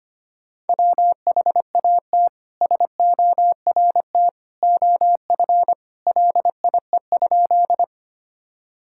Morse code, heard as a keyed tone, about 25 wpm, 700 Hz. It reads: WHAT SORT OF LIE?
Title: Sentences from Top 1600 Words 25wpm